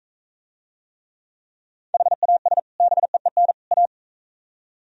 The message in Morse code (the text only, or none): HAS BEEN A